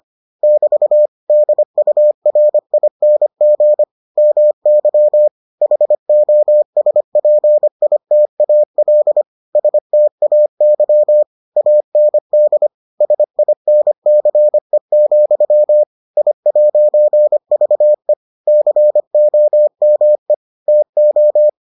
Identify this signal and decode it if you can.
Morse code, heard as a keyed tone, about 25 wpm, 600 Hz. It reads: <BT> DURING MY HOSPITAL STAY AND SINCE, I'VE COME TO